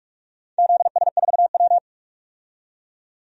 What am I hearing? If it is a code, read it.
Morse code, heard as a keyed tone, about 44 wpm, 700 Hz. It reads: ZS4W
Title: DXCC-100 Call Signs - 4 Characters - 45wpm